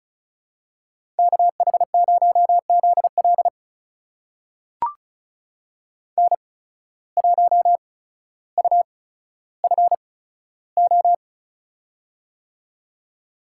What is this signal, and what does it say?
Morse code, heard as a keyed tone, about 36 wpm, spaced as for 15 wpm, 700 Hz. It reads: KH0ZL N1UFO
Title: USA Call Signs - 8x Character Spacing - 35wpm